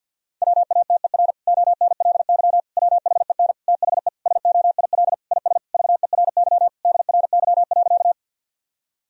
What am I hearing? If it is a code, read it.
Morse code, heard as a keyed tone, about 51 wpm, 700 Hz. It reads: WATER ONLY WHEN THE SOIL IS VERY DRY.
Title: Sentences from Top 1000 Words - Review Part 4 - 50wpm